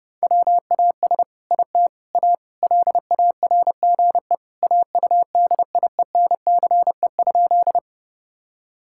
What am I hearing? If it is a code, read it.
Morse code, heard as a keyed tone, about 30 wpm, 700 Hz. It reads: WAS IT A LARGE AUDIENCE?
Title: Sentences from Top 1000 Words - Review Part 5 - 30wpm